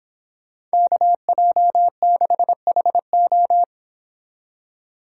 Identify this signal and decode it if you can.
Morse code, heard as a keyed tone, about 26 wpm, 700 Hz. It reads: KJ6HO